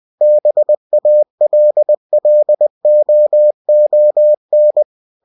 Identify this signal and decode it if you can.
Morse code, heard as a keyed tone, about 20 wpm, 600 Hz. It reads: BALLOON
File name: cw-books / winnie-the-pooh / winnie-20wpm-page-0058